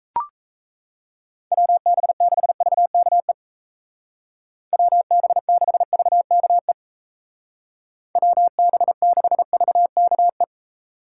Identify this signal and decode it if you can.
Morse code, heard as a keyed tone, about 35 wpm, 700 Hz. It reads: WB6VKE WB6VKE WB6VKE